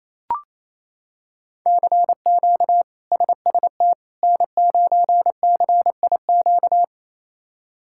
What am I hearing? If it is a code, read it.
Morse code, heard as a keyed tone, about 28 wpm, 700 Hz. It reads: CQ SST N9CIQ
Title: K1USN SST QSO Practice - 28wpm